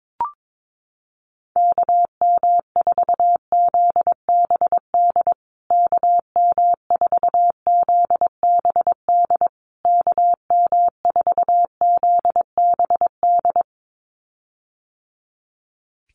Morse code, heard as a keyed tone, about 22 wpm, 700 Hz. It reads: KM4ZBD KM4ZBD KM4ZBD
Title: USA Call Signs Repeated 3x 22wpm